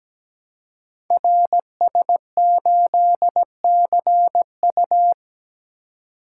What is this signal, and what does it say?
Morse code, heard as a keyed tone, about 17 wpm, 700 Hz. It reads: RS8CU
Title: Code Groups - 5 Letter:Numbers - 17wpm